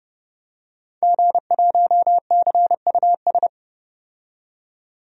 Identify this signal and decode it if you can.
Morse code, heard as a keyed tone, about 30 wpm, 700 Hz. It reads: G1CUS
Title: DXCC-100 Call Signs - 5 Characters - 30wpm